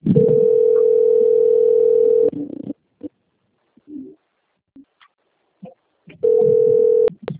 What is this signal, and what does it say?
Ringback tone (North America), 440 + 480 Hz.